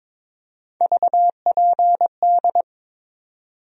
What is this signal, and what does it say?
Morse code, heard as a keyed tone, about 22 wpm, 700 Hz. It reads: VPD
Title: USA Call Sign Postfixes - 3 Letters 22wpm